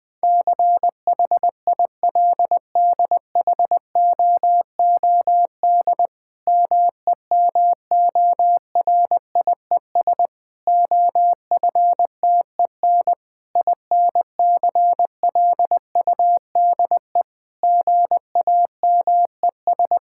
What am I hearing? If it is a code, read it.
Morse code, heard as a keyed tone, about 19 wpm, 700 Hz. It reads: CHILDHOOD MEMORIES OFTEN INCLUDE GAMES
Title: Difficult Sentences with Hints - 20wpm